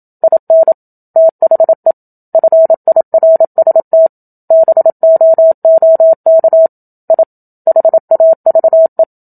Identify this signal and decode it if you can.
Morse code, heard as a keyed tone, about 27 wpm, 650 Hz. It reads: IN THE FIRST BOOK I HAVE
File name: Book Two / 01-Chapter 1 - 27 WPM